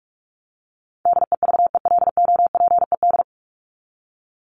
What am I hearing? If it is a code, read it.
Morse code, heard as a keyed tone, about 44 wpm, 700 Hz. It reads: DEVELOPED
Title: ED Suffix - Top 100 Words with Base Words 45wpm